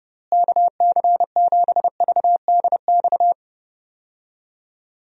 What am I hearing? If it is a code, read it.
Morse code, heard as a keyed tone, about 30 wpm, 700 Hz. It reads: KC7VDX
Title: USA Call Signs - 6 Characters - Repeated 3x - 30wpm